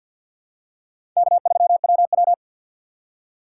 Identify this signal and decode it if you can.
Morse code, heard as a keyed tone, about 52 wpm, 700 Hz. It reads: K3WW
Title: USA Call Signs - 3x Character Spacing - 50wpm